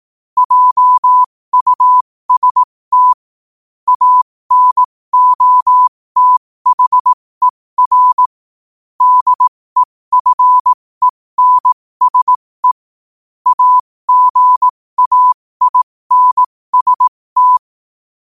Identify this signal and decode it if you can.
Morse code, heard as a keyed tone, about 18 wpm, spaced as for 15 wpm, 1000 Hz. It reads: JUST ANOTHER DEFENSE AGAINST
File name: qotdfiles / 15wpm